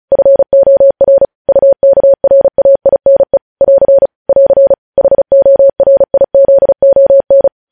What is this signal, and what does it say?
Morse code, heard as a keyed tone, about 36 wpm, 550 Hz. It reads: FOR UKRAINE <AR> <AR> HORIZON
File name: podcast / News headlines-1x-35x35